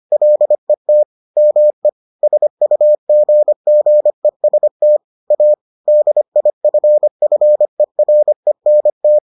Morse code, heard as a keyed tone, about 25 wpm, 600 Hz. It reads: LET ME SUGGEST A DIFFERENT